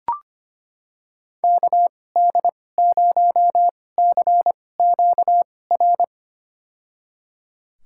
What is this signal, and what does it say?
Morse code, heard as a keyed tone, about 25 wpm, spaced as for 18 wpm, 690 Hz. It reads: KD0CQR
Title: USA Call Signs - 6 Characters - 2x Character Spacing - 25wpm